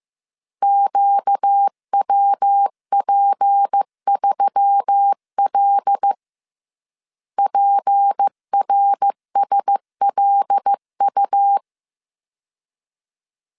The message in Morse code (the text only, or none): QWP3L PRSLU